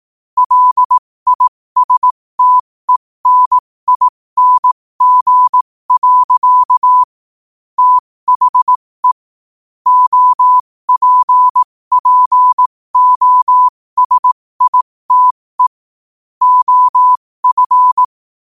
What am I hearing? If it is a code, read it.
Morse code, heard as a keyed tone, about 18 wpm, spaced as for 15 wpm, 1000 Hz. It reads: LISTENING. THE OPPOSITE OF
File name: qotdfiles / 15wpm